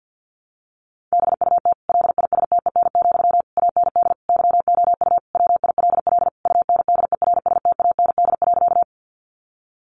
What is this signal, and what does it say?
Morse code, heard as a keyed tone, about 48 wpm, 700 Hz. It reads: BUT LISTEN, AND YOU WILL UNDERSTAND.
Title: Sentences from Top 600 Words 50wpm